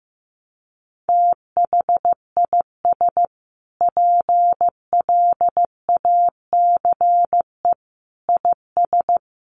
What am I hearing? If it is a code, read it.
Morse code, heard as a keyed tone, about 15 wpm, 700 Hz. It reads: THIS PLACE IS
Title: Sentences from Top 1200 Words 15wpm